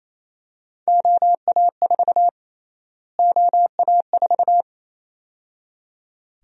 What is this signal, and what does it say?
Morse code, heard as a keyed tone, about 28 wpm, 700 Hz. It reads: OA4 OA4
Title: USA Call Sign Prefixes - Repeated 2x - 28wpm